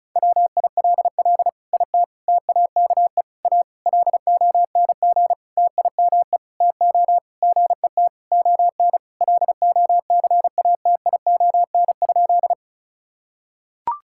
Morse code, heard as a keyed tone, about 34 wpm, 700 Hz. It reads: WILL IT TAKE A LONG TIME TO GET ON LOCATION?
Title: Sentences from Top 1000 Words - Review Part 6 - 35wpm